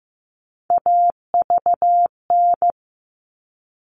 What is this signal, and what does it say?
Morse code, heard as a keyed tone, about 15 wpm, 700 Hz. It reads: AVN
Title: All Cut Numbers - 3-Digit 15wpm